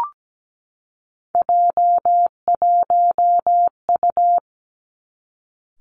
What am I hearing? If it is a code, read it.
Morse code, heard as a keyed tone, about 17 wpm, 700 Hz. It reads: J1U
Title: Three Challenging Letter-Number Combinations 17wpm